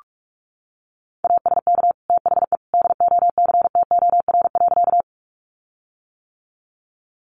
Morse code, heard as a keyed tone, about 44 wpm, 700 Hz. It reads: ASK THE DOCTOR.
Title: Sentences from Top 1500 Words 45wpm